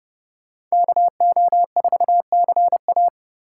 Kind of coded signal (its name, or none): Morse code